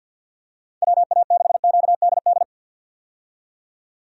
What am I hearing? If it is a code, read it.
Morse code, heard as a keyed tone, about 51 wpm, 700 Hz. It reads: WA6QDD
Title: USA Call Signs - 6 Characters - 50wpm